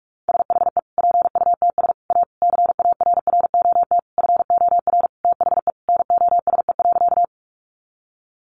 Morse code, heard as a keyed tone, about 44 wpm, 700 Hz. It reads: SHE PUTS A CARROT FOR THE NOSE.
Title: Easy Phrases 45wpm